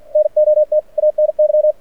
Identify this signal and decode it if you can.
Morse code, heard as a keyed tone, about 53 wpm, 610 Hz. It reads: NOT ANY
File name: archived / CRSnet